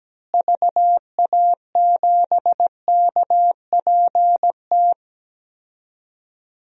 Morse code, heard as a keyed tone, about 17 wpm, 700 Hz. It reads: VA7KPT
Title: DXCC-100 Call Signs - 6 Characters - Repeated 2x - 17wpm